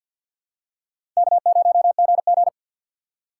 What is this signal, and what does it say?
Morse code, heard as a keyed tone, about 52 wpm, 700 Hz. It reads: K0GG
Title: USA Call Signs - 4 Characters - 2x Character Spacing - 50wpm